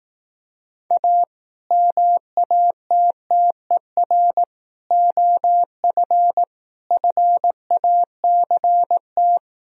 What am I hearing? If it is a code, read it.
Morse code, heard as a keyed tone, about 18 wpm, 700 Hz. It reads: A MATTER OF FACT
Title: Speed-Racing - Sets of 4 Words 15wpm